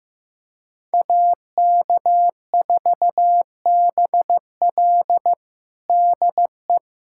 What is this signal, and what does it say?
Morse code, heard as a keyed tone, about 15 wpm, 700 Hz. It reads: AK4BL DE
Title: USA Callsign DE Callsign 15wpm